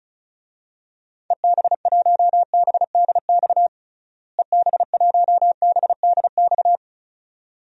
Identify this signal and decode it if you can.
Morse code, heard as a keyed tone, about 36 wpm, 700 Hz. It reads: EB1BDX EB1BDX
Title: DXCC-100 Call Signs - 6 Characters - Repeated 2x - 35wpm